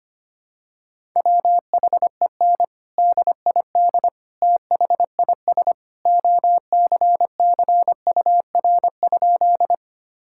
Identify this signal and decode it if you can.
Morse code, heard as a keyed tone, about 25 wpm, 700 Hz. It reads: WHEN DID THIS OCCUR?
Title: Sentences from Top 1000 Words - Review Part 6 - 25wpm